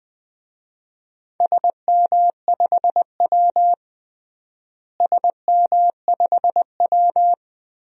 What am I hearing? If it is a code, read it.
Morse code, heard as a keyed tone, about 20 wpm, 700 Hz. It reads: SM5W SM5W
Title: DXCC-100 Call Signs - 4 Characters - Repeated 3x - 20wpm